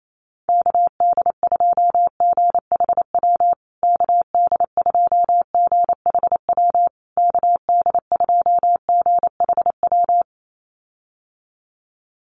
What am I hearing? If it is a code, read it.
Morse code, heard as a keyed tone, about 28 wpm, 700 Hz. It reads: KD2GHW KD2GHW KD2GHW
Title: USA Call Signs Repeated 3x 28wpm